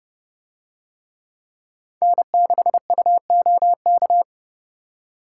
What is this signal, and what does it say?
Morse code, heard as a keyed tone, about 30 wpm, 700 Hz. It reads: N6UOK